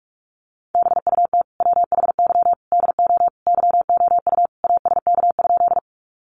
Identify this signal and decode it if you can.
Morse code, heard as a keyed tone, about 44 wpm, 700 Hz. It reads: BUT WHY DO YOU ASK?
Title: Sentences from Top 200 Words 45wpm